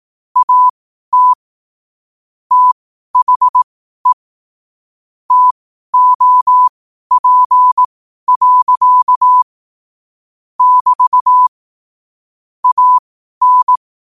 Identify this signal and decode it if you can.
Morse code, heard as a keyed tone, about 18 wpm, spaced as for 12 wpm, 1000 Hz. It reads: AT THE TOP. <BT> AN
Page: New quotes every day in morse code at 12 Words per minute.